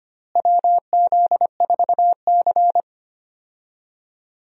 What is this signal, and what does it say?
Morse code, heard as a keyed tone, about 25 wpm, 700 Hz. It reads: WZ4C
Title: USA Call Signs - 4 Characters - Repeated 2x - 25wpm